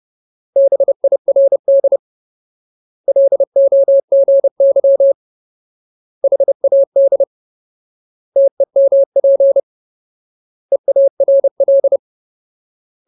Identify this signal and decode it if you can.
Morse code, heard as a keyed tone, about 30 wpm, 550 Hz. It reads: BIRD LOGY HAD TEMP EARL
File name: Ultimate Word List / 30WPM Ultimate